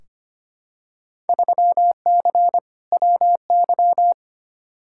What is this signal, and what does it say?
Morse code, heard as a keyed tone, about 25 wpm, 700 Hz. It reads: 3C WY
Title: Field Day Exchanges Encore - 25wpm